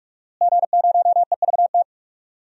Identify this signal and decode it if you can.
Morse code, heard as a keyed tone, about 47 wpm, 700 Hz. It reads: G0EVT
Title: DXCC-100 Call Signs - 5 Characters - Repeated 3x - 45wpm